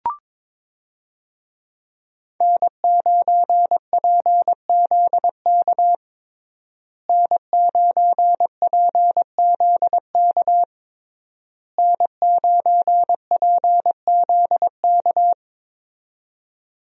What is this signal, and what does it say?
Morse code, heard as a keyed tone, about 22 wpm, 690 Hz. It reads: N9PZK N9PZK N9PZK
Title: USA Call Signs - 5 Characters - Repeated 3x - 22wpm